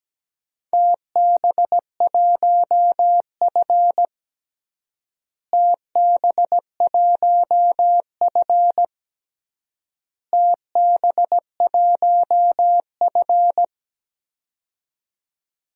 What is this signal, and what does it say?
Morse code, heard as a keyed tone, about 17 wpm, 700 Hz. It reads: TB1F TB1F TB1F
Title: DXCC-100 Call Signs - 4 Characters - Repeated 3x - 17wpm